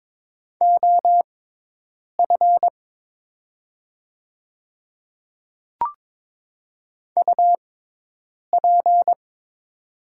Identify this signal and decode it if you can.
Morse code, heard as a keyed tone, about 22 wpm, spaced as for 8 wpm, 700 Hz. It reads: OF UP
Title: 6x Character Spacing - Top 100 Words - 22wpm